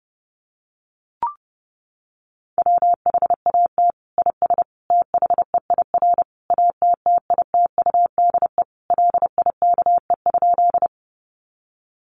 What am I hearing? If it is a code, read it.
Morse code, heard as a keyed tone, about 30 wpm, 700 Hz. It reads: WHAT IS THEIR ATTITUDE LIKE?
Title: Sentences from Top 1300 Words 30wpm